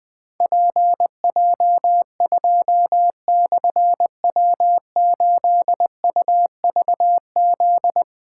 Morse code, heard as a keyed tone, about 21 wpm, 700 Hz. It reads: PJ2/W8UVZ
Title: Difficult Call Signs 20wpm